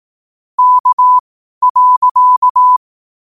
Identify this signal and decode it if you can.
Morse code, heard as a keyed tone, about 18 wpm, spaced as for 12 wpm, 1000 Hz. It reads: K.
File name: qotdfiles / 12wpm